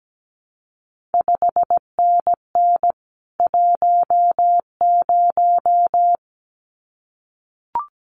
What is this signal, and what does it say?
Morse code, heard as a keyed tone, about 17 wpm, 700 Hz. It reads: 5NN 10
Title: CQ WW DX Contest Exchange Encore - 17wpm